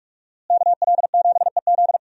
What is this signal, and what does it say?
Morse code, heard as a keyed tone, about 44 wpm, 700 Hz. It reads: KL7EZ
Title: DXCC-100 Call Signs - 5 Characters - Repeated 3x - 45wpm